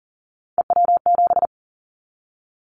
Morse code, heard as a keyed tone, about 41 wpm, 700 Hz. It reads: EW7
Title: Callsign Prefixes for Top 100 DXCC - 40wpm